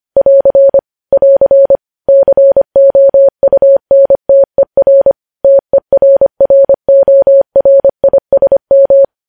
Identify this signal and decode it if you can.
Morse code, heard as a keyed tone, about 25 wpm, 550 Hz. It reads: <AR> <AR> COUNTER TERRORISM
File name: podcast / News headlines-1x-25x25